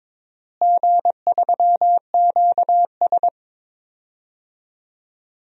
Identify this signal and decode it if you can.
Morse code, heard as a keyed tone, about 22 wpm, 700 Hz. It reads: G3QS
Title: DXCC-100 Call Signs - 4 Characters - Repeated 2x - 22wpm